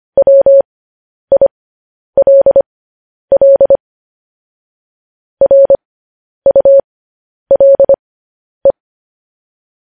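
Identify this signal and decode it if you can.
Morse code, heard as a keyed tone, about 25 wpm, spaced as for 10 wpm, 550 Hz. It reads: WILL RULE
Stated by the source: Morse code transcription of news headlines.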